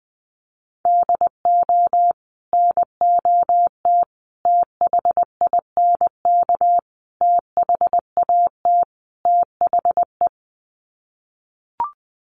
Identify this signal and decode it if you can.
Morse code, heard as a keyed tone, about 20 wpm, 700 Hz. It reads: DO NOT THINK THAT THE
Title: Sets of 5 Words Encore2 20wpm